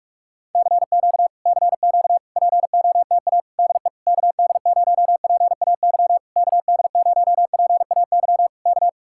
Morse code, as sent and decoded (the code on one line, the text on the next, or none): -.-. --.- / -.-. --.- / .--. --- - .- / -.. / -.- -.. ----- .--. .- -.-- / -.- -.. ----- .--. .- -.-- / -.-
CQ CQ POTA DE KD0PAY KD0PAY K